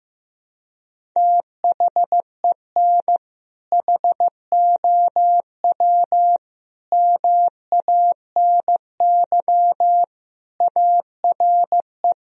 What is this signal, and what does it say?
Morse code, heard as a keyed tone, about 15 wpm, 700 Hz. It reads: THEN HOW MANY ARE
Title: Sentences from Top 1000 Words - Review Part 1 - 15wpm